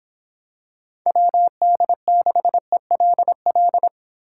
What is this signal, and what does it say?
Morse code, heard as a keyed tone, about 26 wpm, 700 Hz. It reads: WD6ELL